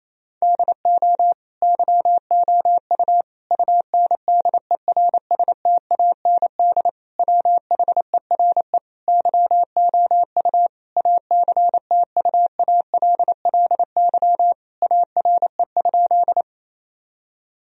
Morse code, heard as a keyed tone, about 28 wpm, 700 Hz. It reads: DO YOU UNDERSTAND WHERE YOU ACTUALLY ARE?
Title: Sentences from Top 1000 Words - Review Part 4 - 28wpm